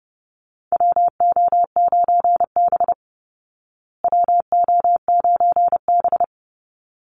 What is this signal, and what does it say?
Morse code, heard as a keyed tone, about 30 wpm, 700 Hz. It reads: WO9B WO9B
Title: Call Signs Repeated 3x - 30wpm